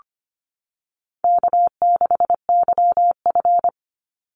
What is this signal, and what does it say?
Morse code, heard as a keyed tone, about 25 wpm, 700 Hz. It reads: K6YF
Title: Easy Call Signs Repeated 3x - 25wpm